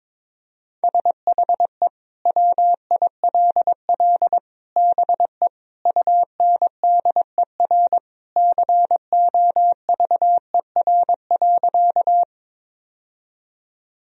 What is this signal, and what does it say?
Morse code, heard as a keyed tone, about 22 wpm, 700 Hz. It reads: SHE WILL BE UNDER COVER.
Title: Sentences from Top 1000 Words - Review Part 1 - 22wpm